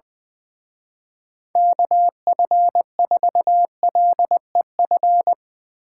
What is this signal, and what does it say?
Morse code, heard as a keyed tone, about 20 wpm, 700 Hz. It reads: KF4LEF
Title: Speed Racing - USA 2x3 Call Signs 20wpm